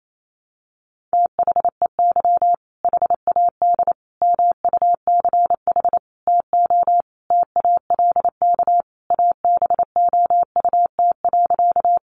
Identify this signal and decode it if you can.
Morse code, heard as a keyed tone, about 28 wpm, 700 Hz. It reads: THEY HAD MUCH TO TALK ABOUT.
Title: Sentences from Top 400 Words 28wpm